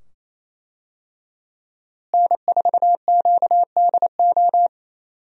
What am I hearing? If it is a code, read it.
Morse code, heard as a keyed tone, about 28 wpm, 700 Hz. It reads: N4QDO
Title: USA Call Signs - 5 Characters - Repeated 2x - 28wpm